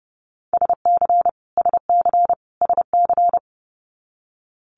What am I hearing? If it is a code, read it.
Morse code, heard as a keyed tone, about 30 wpm, 700 Hz. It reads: SC SC SC
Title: Multi-letter Phonemes Repeated 3x 30wpm